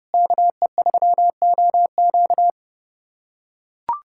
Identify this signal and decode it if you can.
Morse code, heard as a keyed tone, about 30 wpm, 700 Hz. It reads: KE3OQ
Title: USA Callsign DE Callsign - 5 Characters - 30wpm